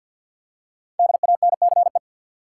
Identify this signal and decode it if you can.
Morse code, heard as a keyed tone, about 51 wpm, 700 Hz. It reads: DANCE
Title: Five Letter Words - 50wpm